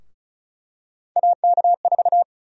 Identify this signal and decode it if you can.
Morse code, heard as a keyed tone, about 34 wpm, 700 Hz. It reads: AK4
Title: USA Call Sign Prefixes - Repeated 2x - 35wpm